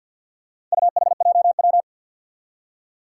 Morse code, heard as a keyed tone, about 53 wpm, 700 Hz. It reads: UHJW
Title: Four Challenging Letter-Number Combinations 50wpm